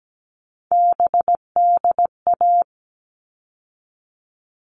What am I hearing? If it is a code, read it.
Morse code, heard as a keyed tone, about 17 wpm, 700 Hz. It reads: BDA
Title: All Cut Numbers - 3-Digit 17wpm